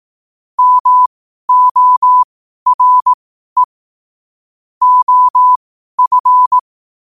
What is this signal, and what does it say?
Morse code, heard as a keyed tone, about 18 wpm, spaced as for 12 wpm, 1000 Hz. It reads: MORE OF
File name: qotdfiles / 12wpm